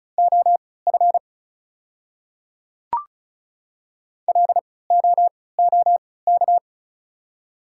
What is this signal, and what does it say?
Morse code, heard as a keyed tone, about 36 wpm, spaced as for 20 wpm, 700 Hz. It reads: OF LOOK
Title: 3x Character Spacing - Top 100 Words - 35wpm